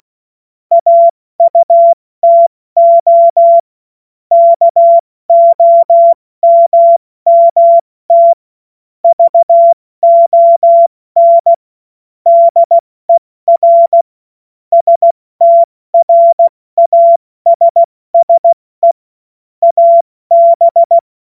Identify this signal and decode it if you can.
Morse code, heard as a keyed tone, about 16 wpm, 680 Hz. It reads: AUTO KOMMT VON DER STRASSE AB
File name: Morsekurs / Witze / Witz 14. Dezember 2021, 70 F80